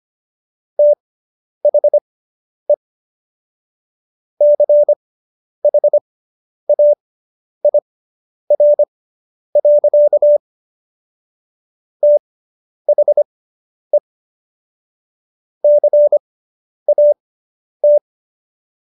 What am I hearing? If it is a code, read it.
Morse code, heard as a keyed tone, about 25 wpm, spaced as for 10 wpm, 590 Hz. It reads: THE CHAIR. THE CAT